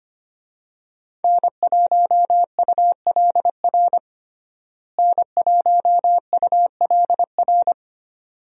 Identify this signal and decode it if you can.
Morse code, heard as a keyed tone, about 25 wpm, 700 Hz. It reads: N1ULR N1ULR
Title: USA Call Signs - 5 Characters - Repeated 2x - 25wpm